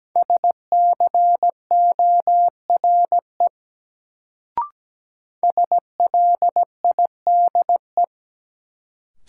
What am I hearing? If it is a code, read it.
Morse code, heard as a keyed tone, about 17 wpm, 700 Hz. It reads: SCORE SLIDE
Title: Sentences from Top 1600 Words - New Word Drill - 17wpm